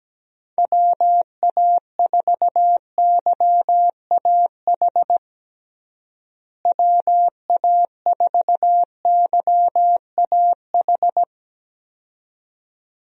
Morse code, heard as a keyed tone, about 17 wpm, 700 Hz. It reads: WA4YAH WA4YAH
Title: USA Call Signs - 6 Characters - Repeated 2x - 17wpm